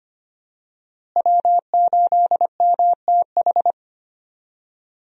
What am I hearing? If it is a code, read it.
Morse code, heard as a keyed tone, about 25 wpm, 700 Hz. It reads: W8MTH
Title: USA Call Signs - 5 Characters - Repeated 2x - 25wpm